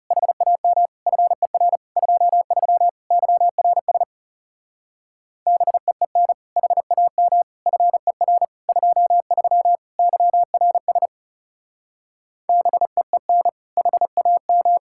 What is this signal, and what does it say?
Morse code, heard as a keyed tone, about 36 wpm, 700 Hz. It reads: HAM FER 23 YRS BEEN HAM FER 23 YRS BEEN HAM
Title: Speed-Racing Common QSO Phrases - 30wpm